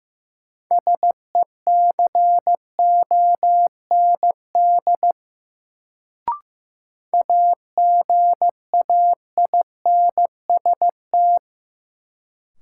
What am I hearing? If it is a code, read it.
Morse code, heard as a keyed tone, about 15 wpm, 700 Hz. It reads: SECOND AGAINST
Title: Sets of 2 Words - New Word Drill - 15wpm